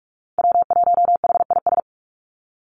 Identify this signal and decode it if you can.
Morse code, heard as a keyed tone, about 44 wpm, 700 Hz. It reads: W1HIS
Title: Call Signs - 45wpm